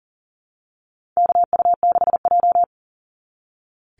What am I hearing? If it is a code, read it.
Morse code, heard as a keyed tone, about 40 wpm, 700 Hz. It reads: KU6J
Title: NA SOTA Activator Callsigns 40wpm